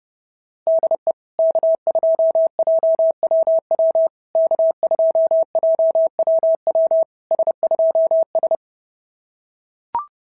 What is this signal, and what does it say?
Morse code, heard as a keyed tone, about 28 wpm, 650 Hz. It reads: DE K2JWW K2JWW S2S
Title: SOTA QSO Practice 30wpm